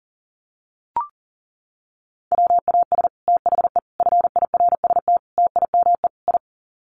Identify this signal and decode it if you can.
Morse code, heard as a keyed tone, about 40 wpm, 700 Hz. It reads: WAS THE FIRST TIME I
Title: Sets of 5 Words Encore 40wpm